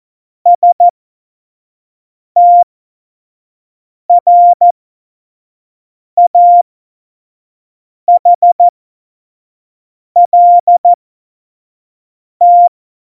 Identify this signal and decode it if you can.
Morse code, heard as a keyed tone, about 14 wpm, spaced as for 5 wpm, 700 Hz. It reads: STRAHLT
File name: Morsekurs / Witze / Witz 15. September 2020, 25 F70